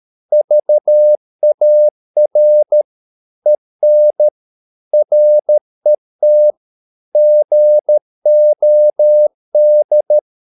Morse code, heard as a keyed tone, about 13 wpm, 600 Hz. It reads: VAR EN RET GOD
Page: Ferie 13 wpm